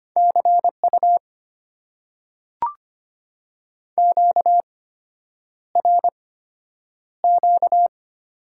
Morse code, heard as a keyed tone, about 25 wpm, spaced as for 9 wpm, 700 Hz. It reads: CU QRQ
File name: Top 90 QSO Elements - 8x Character Spacing - 25wpm